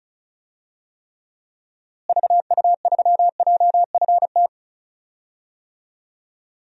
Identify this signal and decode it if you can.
Morse code, heard as a keyed tone, about 34 wpm, 700 Hz. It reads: VU3JFT
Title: DXCC-100 Call Signs - 6 Characters - Repeated 3x - 35wpm